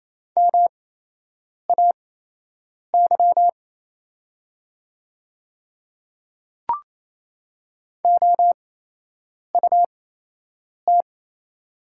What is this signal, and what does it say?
Morse code, heard as a keyed tone, about 28 wpm, spaced as for 8 wpm, 700 Hz. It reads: MAY OUT
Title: 8x Character Spacing - Top 100 Words - 28wpm